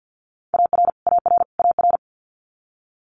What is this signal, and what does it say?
Morse code, heard as a keyed tone, about 50 wpm, 700 Hz. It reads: AR AR AR
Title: Multi-letter Phonemes Repeated 3x 50wpm